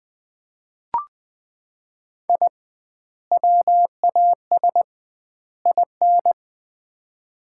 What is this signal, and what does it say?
Morse code, heard as a keyed tone, about 20 wpm, 700 Hz. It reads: I WAS IN
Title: 2x Word-Spacing - Sets of 3 Words - 20wpm